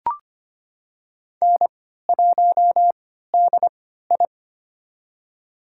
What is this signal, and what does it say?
Morse code, heard as a keyed tone, about 25 wpm, spaced as for 14 wpm, 700 Hz. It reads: N1DI
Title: USA Call Signs - 4 Characters - 3x Character Spacing - 25wpm